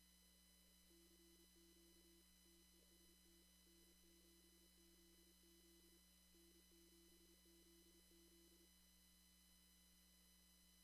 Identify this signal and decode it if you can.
Morse code, heard as a keyed tone, about 28 wpm, 340 Hz. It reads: CQ TEST N7ZZ N7ZZ
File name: CW / kt5j wpx cw-op-k5pi